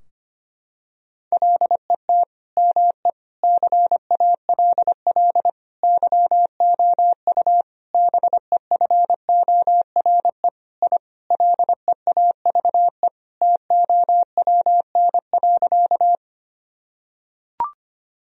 Morse code, heard as a keyed tone, about 25 wpm, 700 Hz. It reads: LET ME CALL YOU BEFORE I LEAVE TOWN.
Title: Sentences from Top 1000 Words - Review Part 2 - 25wpm